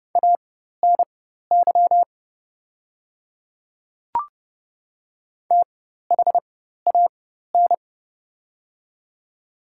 Morse code, heard as a keyed tone, about 30 wpm, spaced as for 14 wpm, 700 Hz. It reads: ANY THAN
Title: 4x Character Spacing - Top 200 Words - 30wpm